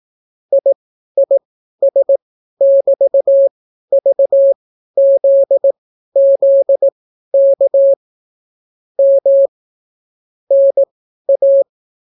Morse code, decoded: IIS<BT>VZZK M NA

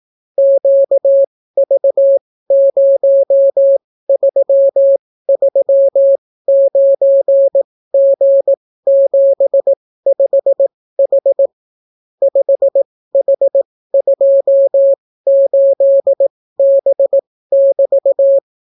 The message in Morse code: QV0339G75H 5H28B<BT>

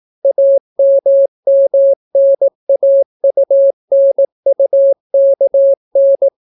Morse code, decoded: AMMNAUNUKN

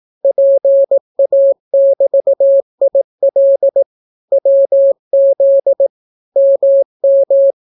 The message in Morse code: PA<BT>IL WZ MM

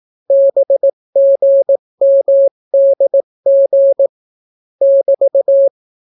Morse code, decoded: BGMDG <BT>

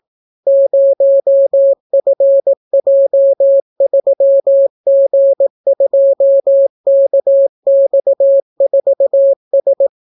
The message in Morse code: 0FJ3G2KX4S

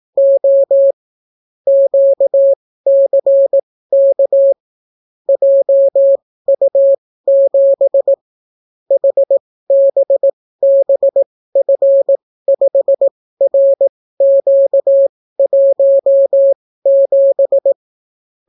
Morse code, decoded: O QCK JU7 HBBF5RQ17

